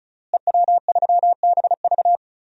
Morse code, heard as a keyed tone, about 36 wpm, 700 Hz. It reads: EW3BV
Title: DXCC-100 Call Signs - 5 Characters - Repeated 2x - 35wpm